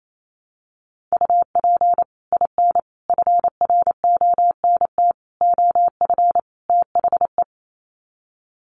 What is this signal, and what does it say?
Morse code, heard as a keyed tone, about 28 wpm, 700 Hz. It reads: UP IN FRONT OF THE
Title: Sets of 5 Words Encore2 28wpm